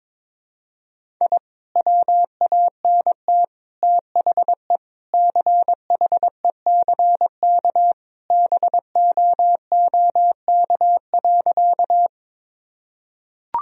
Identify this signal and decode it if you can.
Morse code, heard as a keyed tone, about 22 wpm, 700 Hz. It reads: I WANT THE CHECK BOOK.
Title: Sentences from Top 1000 Words - Review Part 3 - 22wpm